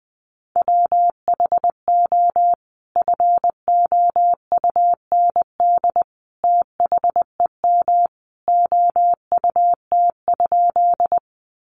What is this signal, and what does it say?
Morse code, heard as a keyed tone, about 20 wpm, 700 Hz. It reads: WHO FOUND THEM OUT?
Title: Sentences from Top 300 Words 20wpm